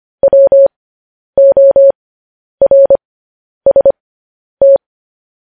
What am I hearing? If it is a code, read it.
Morse code, heard as a keyed tone, about 25 wpm, spaced as for 10 wpm, 550 Hz. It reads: WORST
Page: news headlines in morse code at 10 wpm